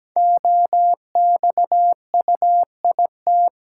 Morse code, heard as a keyed tone, about 17 wpm, 700 Hz. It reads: OXUIT
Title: Code Groups - 5 Letters - 17wpm